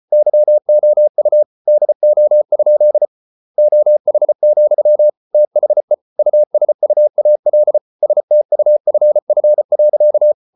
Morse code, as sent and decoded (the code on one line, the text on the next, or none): -.-- --- ..- / -.. --- ..--.. / --- .... --..-- / - .... / ..- ... ..- .- .-.. / ... - ..- ..-. ..-. .-.-.-
YOU DO? OH, THE USUAL STUFF.